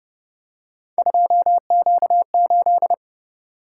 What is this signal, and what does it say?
Morse code, heard as a keyed tone, about 26 wpm, 700 Hz. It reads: 2Q8